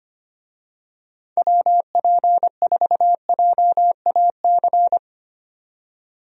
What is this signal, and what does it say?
Morse code, heard as a keyed tone, about 25 wpm, 700 Hz. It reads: WP4JAC
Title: USA Call Signs - 6 Characters - 25wpm